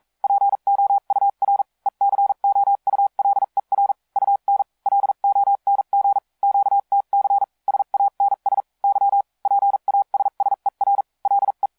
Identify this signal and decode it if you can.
Morse code, heard as a keyed tone, about 42 wpm, spaced as for 36 wpm, 800 Hz. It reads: POUR ECOULER UN LONG QTC SANS Y PASSER LE